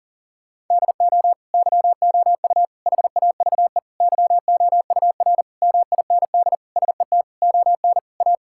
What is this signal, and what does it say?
Morse code, heard as a keyed tone, about 40 wpm, 700 Hz. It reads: DO YOU HAVE YOUR MIND SET ON A
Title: Sentences from Top 1000 Words - Review Part 5 - 40wpm